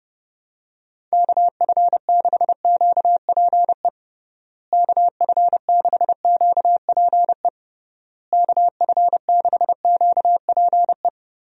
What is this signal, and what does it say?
Morse code, heard as a keyed tone, about 30 wpm, 700 Hz. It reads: KF6QPE KF6QPE KF6QPE
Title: USA Call Signs - 6 Characters - Repeated 3x - 30wpm